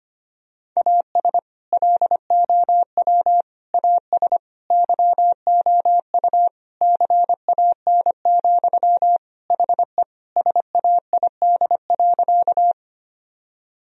Morse code, heard as a keyed tone, about 25 wpm, 700 Hz. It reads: AS LOW AS YOU CAN, HE SAID.
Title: Sentences from Top 1000 Words - Review Part 1 - 25wpm